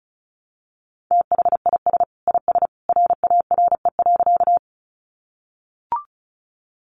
Morse code, heard as a keyed tone, about 34 wpm, 700 Hz. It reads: THIS IS RARE.
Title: Sentences from Top 1100 Words 35wpm